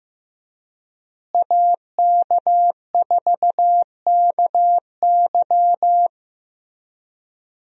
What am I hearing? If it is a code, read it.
Morse code, heard as a keyed tone, about 15 wpm, 700 Hz. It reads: AK4KY
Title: USA Call Signs - 5 Characters - Repeated 3x - 15wpm